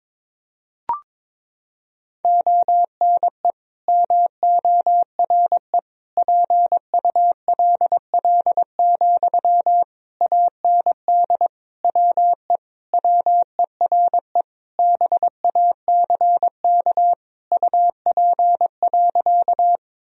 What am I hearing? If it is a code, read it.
Morse code, heard as a keyed tone, about 22 wpm, 700 Hz. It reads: ONE MORE PULL, AND WE WERE BACK UP.
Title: Sentences from Top 1000 Words - Review Part 2 - 22wpm